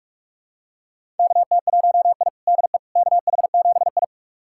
Morse code, heard as a keyed tone, about 42 wpm, 700 Hz. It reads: KT1I DE KH7I
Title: USA Callsign DE Callsign - 4 Characters - 45wpm